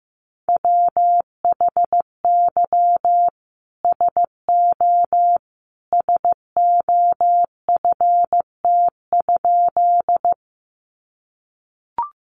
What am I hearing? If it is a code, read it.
Morse code, heard as a keyed tone, about 15 wpm, 700 Hz. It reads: WHY SO SOFT?
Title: Sentences from Top 1500 Words 15wpm